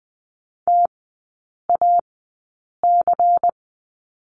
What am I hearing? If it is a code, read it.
Morse code, heard as a keyed tone, about 20 wpm, spaced as for 8 wpm, 700 Hz. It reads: TAC
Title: 093 Lesson 16e - Letter c - Three Characters 20wpm